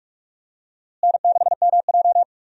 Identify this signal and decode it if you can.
Morse code, heard as a keyed tone, about 46 wpm, 700 Hz. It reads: N6MJ
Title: USA Call Signs - 4 Characters - Repeated 3x - 45wpm